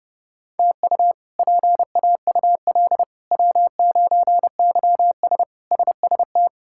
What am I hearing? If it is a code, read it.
Morse code, heard as a keyed tone, about 31 wpm, 700 Hz. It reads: TU PAUL W9YS SST
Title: K1USN SST QSO Practice - 30wpm